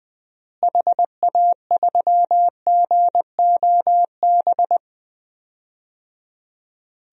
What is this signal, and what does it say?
Morse code, heard as a keyed tone, about 20 wpm, 700 Hz. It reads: HA3GOB
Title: DXCC-100 Call Signs - 6 Characters - Repeated 3x - 20wpm